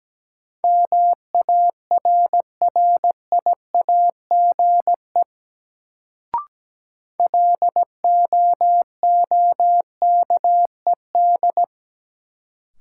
Morse code, heard as a keyed tone, about 17 wpm, 700 Hz. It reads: MARRIAGE LOOKED
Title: Sentences from Top 800 Words - New Word Drill - 17wpm